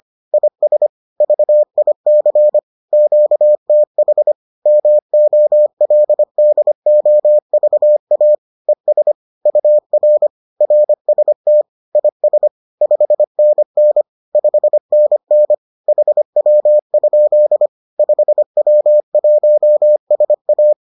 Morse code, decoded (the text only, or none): IS VIC QTH MOLDOVA ES UR RST IS 5NN 5NN HW? 5W1SA